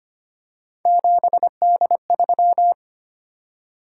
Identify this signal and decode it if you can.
Morse code, heard as a keyed tone, about 25 wpm, 700 Hz. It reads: 7D3
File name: Three Letter-Number Combinations - 25wpm